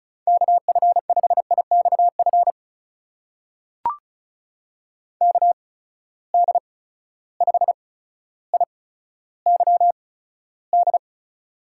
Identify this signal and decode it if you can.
Morse code, heard as a keyed tone, about 36 wpm, spaced as for 15 wpm, 700 Hz. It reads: KF5IXF KD5IYD
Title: USA Call Signs - 6 Characters - 8x Character Spacing - 35wpm